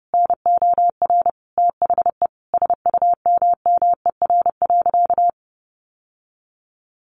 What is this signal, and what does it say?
Morse code, heard as a keyed tone, about 30 wpm, 700 Hz. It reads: NOR THE SUMMER.
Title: Sentences from Top 1500 Words 30wpm